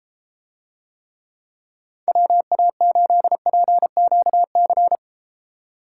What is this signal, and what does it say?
Morse code, heard as a keyed tone, about 33 wpm, 700 Hz. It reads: WA8PQC